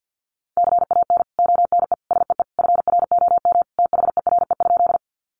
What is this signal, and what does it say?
Morse code, heard as a keyed tone, about 50 wpm, 700 Hz. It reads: CAN ONE SEE FROM THERE?
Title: Sentences from Top 100 Words 50wpm